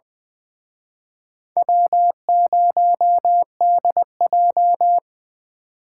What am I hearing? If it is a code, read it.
Morse code, heard as a keyed tone, about 20 wpm, 700 Hz. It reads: W0DJ
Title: Speed Racing - USA 1x2 Call Signs 17wpm